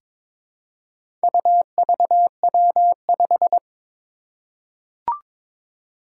Morse code, heard as a keyed tone, about 22 wpm, 700 Hz. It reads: UVW5
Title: Four Challenging Letter-Number Combinations 22wpm